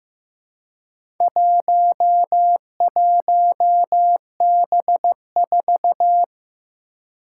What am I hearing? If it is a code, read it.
Morse code, heard as a keyed tone, about 15 wpm, 700 Hz. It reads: 11B4
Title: Four Challenging Letter-Number Combinations 15wpm